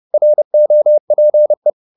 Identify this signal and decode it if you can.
Morse code, heard as a keyed tone, about 30 wpm, 600 Hz. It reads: ROPE
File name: the old man and the sea / the old man and the sea 1 30wpm